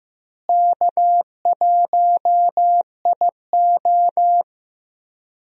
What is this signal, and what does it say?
Morse code, heard as a keyed tone, about 15 wpm, 700 Hz. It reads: K1IO
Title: USA Call Signs - 4 Characters - Repeated 2x - 15wpm